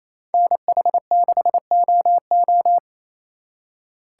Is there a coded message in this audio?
Yes: Morse code from 0.3 to 2.8 s, heard as a keyed tone, about 36 wpm, 700 Hz. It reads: NH6OO